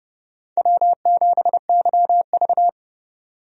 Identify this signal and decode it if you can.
Morse code, heard as a keyed tone, about 30 wpm, 700 Hz. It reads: W7YV
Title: USA Call Signs - 4 Characters - 30wpm